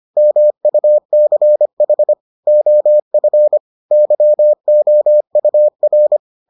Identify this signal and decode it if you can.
Morse code, heard as a keyed tone, about 25 wpm, 600 Hz. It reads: MUCH OF YOUR